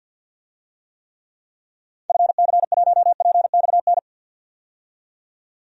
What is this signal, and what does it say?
Morse code, heard as a keyed tone, about 49 wpm, 700 Hz. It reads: FC1PXN
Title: DXCC-100 Call Signs - 6 Characters - 50wpm